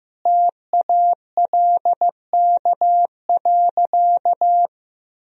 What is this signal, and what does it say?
Morse code, heard as a keyed tone, about 15 wpm, 700 Hz. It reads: TALK.
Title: Sentences from Top 1000 Words - Review Part 4 - 15wpm